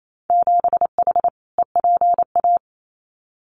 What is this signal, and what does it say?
Morse code, heard as a keyed tone, about 28 wpm, 690 Hz. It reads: 7H EPA
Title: Winter Field Day Exchanges Encore 28wpm